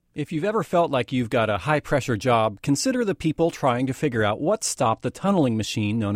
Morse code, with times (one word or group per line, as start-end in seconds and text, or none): none